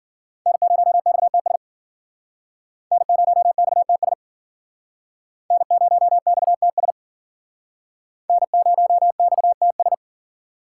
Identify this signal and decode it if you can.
Morse code, heard as a keyed tone, about 51 wpm, 700 Hz. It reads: N0XTS N0XTS N0XTS N0XTS